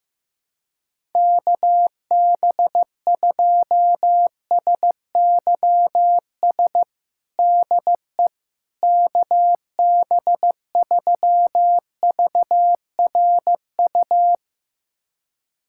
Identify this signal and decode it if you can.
Morse code, heard as a keyed tone, about 15 wpm, 700 Hz. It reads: KB2SYS DE KB3VRU
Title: USA Callsign DE Callsign - 6 Characters - 15wpm